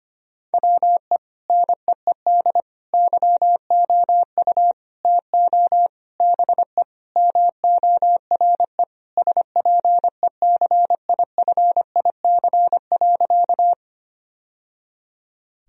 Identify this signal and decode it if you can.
Morse code, heard as a keyed tone, about 25 wpm, 700 Hz. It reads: WE NEED YOU TO BE MORE SPECIFIC.
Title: Sentences from Top 1000 Words - Review Part 3 - 25wpm